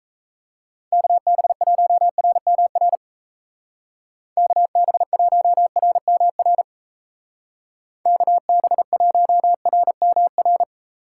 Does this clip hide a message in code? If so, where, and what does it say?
Yes: Morse code from 0.9 to 10.6 s, heard as a keyed tone, about 35 wpm, 700 Hz. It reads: KB1RMR KB1RMR KB1RMR